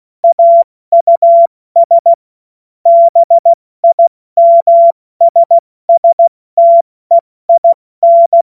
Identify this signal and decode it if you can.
Morse code, heard as a keyed tone, about 16 wpm, 680 Hz. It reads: AUS BIMSSTEIN
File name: Morsekurs / Witze / Witz 18. Januar 2022, 70 F80